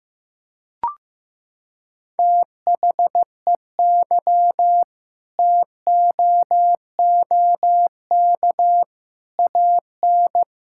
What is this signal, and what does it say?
Morse code, heard as a keyed tone, about 15 wpm, 700 Hz. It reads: THEY TOOK AN
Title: Sentences from Top 1000 Words - Review Part 6 - 15wpm